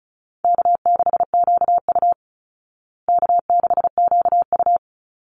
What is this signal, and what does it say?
Morse code, heard as a keyed tone, about 34 wpm, 700 Hz. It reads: K6QU K6QU
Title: Easy Call Signs Repeated 2x - 35wpm